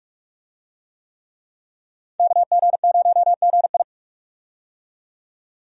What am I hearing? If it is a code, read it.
Morse code, heard as a keyed tone, about 43 wpm, 700 Hz. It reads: KG0GI
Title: USA Call Signs - 5 Characters - 45wpm